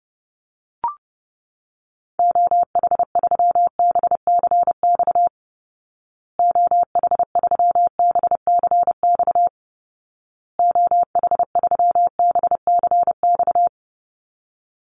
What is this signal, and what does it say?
Morse code, heard as a keyed tone, about 28 wpm, 700 Hz. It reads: OH3BCX OH3BCX OH3BCX
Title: Call Signs Repeated 2x - 30wpm